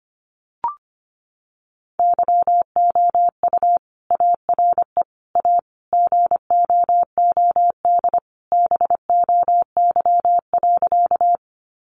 Morse code, heard as a keyed tone, about 25 wpm, 700 Hz. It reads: YOU ARE A GOOD BOY.
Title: Sentences from Top 200 Words 25wpm